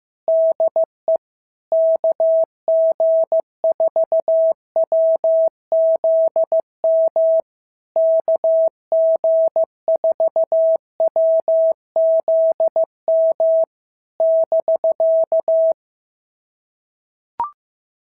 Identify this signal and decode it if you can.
Morse code, heard as a keyed tone, about 15 wpm, 650 Hz. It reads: DE KG4WZM KG4WZM <BK>
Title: SOTA QSO Practice 15wpm